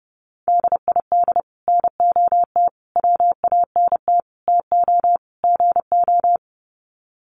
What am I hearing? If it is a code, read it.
Morse code, heard as a keyed tone, about 28 wpm, 700 Hz. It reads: DID NOT WANT TO GO
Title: Sets of 5 Words 30wpm